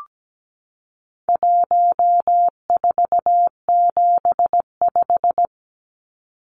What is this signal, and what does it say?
Morse code, heard as a keyed tone, about 17 wpm, 700 Hz. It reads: 1475
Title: Numbers - 4-Digit 17wpm